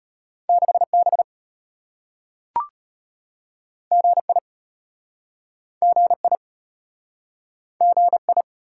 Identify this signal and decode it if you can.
Morse code, heard as a keyed tone, about 34 wpm, 700 Hz. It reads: 6B GI GI GI